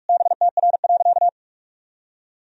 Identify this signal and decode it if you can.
Morse code, heard as a keyed tone, about 46 wpm, 700 Hz. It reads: BTR.
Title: Sentences from Top 400 Words and Abbreviations 45wpm